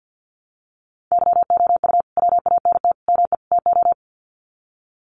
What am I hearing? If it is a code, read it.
Morse code, heard as a keyed tone, about 50 wpm, 700 Hz. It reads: YOU WANT ME TO
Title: Sets of 4 Words 50wpm